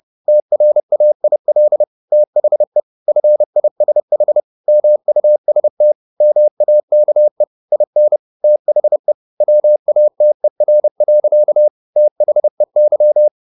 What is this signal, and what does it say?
Morse code, heard as a keyed tone, about 30 wpm, 600 Hz. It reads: TRAIL THE FISH MUST MAKE IN THE WATER. THEY